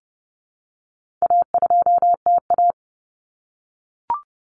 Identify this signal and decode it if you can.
Morse code, heard as a keyed tone, about 30 wpm, 700 Hz. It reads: A2TA
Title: Common Cut Numbers - 4-Digit 30wpm